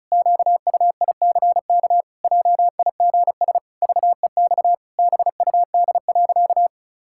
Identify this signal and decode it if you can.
Morse code, heard as a keyed tone, about 36 wpm, 700 Hz. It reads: QUICK JIGS VEX BUD.
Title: Pangram Challenge v2 - 35wpm